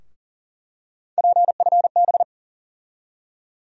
Morse code, heard as a keyed tone, about 40 wpm, 700 Hz. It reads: PFB
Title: USA Call Sign Postfixes - 3 Letters 40wpm